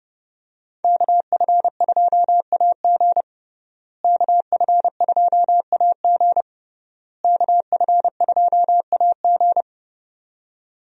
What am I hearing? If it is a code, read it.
Morse code, heard as a keyed tone, about 30 wpm, 700 Hz. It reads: KF2AG KF2AG KF2AG
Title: USA Call Signs - 5 Characters - Repeated 3x - 30wpm